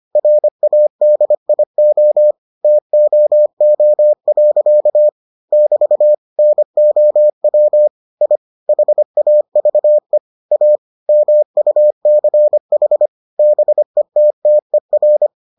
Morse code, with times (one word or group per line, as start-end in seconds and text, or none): 0.2-2.3 RADIO
2.6-5.1 TOO.
5.5-6.1 <BT>
6.4-7.9 NOW
8.2-8.4 I
8.7-10.2 HAVE
10.5-10.8 A
11.1-13.1 MUCH
13.4-15.3 BETTER